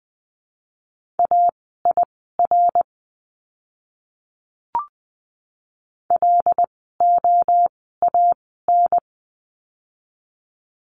Morse code, heard as a keyed tone, about 20 wpm, spaced as for 14 wpm, 700 Hz. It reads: AIR LOAN
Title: 075 Lesson 13f - Letter L - Words 20wpm